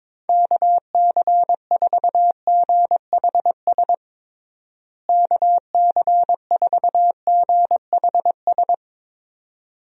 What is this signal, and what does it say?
Morse code, heard as a keyed tone, about 22 wpm, 700 Hz. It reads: KC4GHS KC4GHS
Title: USA Call Signs - 6 Characters - Repeated 2x - 22wpm